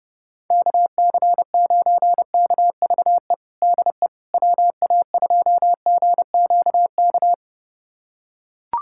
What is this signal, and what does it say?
Morse code, heard as a keyed tone, about 29 wpm, 700 Hz. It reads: KC9KVE DE WA2GQK
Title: USA Callsign DE Callsign - 6 Characters - 30wpm